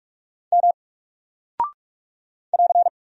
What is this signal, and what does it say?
Morse code, heard as a keyed tone, about 44 wpm, 700 Hz. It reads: M <AR>
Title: Single Character (Letter:Number:Punctuation:Prosign) 45wpm